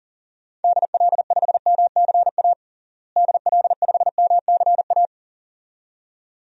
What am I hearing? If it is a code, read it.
Morse code, heard as a keyed tone, about 40 wpm, 700 Hz. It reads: DL5MCA DL5MCA
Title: DXCC-100 Call Signs - 6 Characters - Repeated 2x - 40wpm